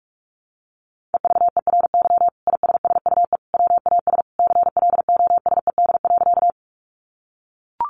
Morse code, heard as a keyed tone, about 44 wpm, 700 Hz. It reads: EVERY ISSUE WAS CLOSED.
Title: Sentences from Top 1400 Words 45wpm